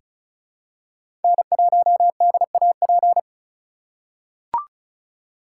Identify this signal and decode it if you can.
Morse code, heard as a keyed tone, about 36 wpm, 700 Hz. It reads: N1DAP
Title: USA Call Signs - 4x Character Spacing - 35wpm